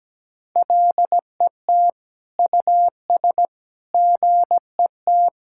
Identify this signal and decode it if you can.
Morse code, heard as a keyed tone, about 17 wpm, 700 Hz. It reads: LET US GET
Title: Sentences from Top 1600 Words 17wpm